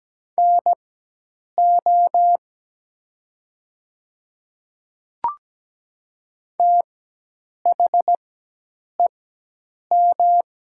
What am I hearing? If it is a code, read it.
Morse code, heard as a keyed tone, about 17 wpm, spaced as for 8 wpm, 700 Hz. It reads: NO THEM
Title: 4x Character Spacing - Top 100 Words - 17wpm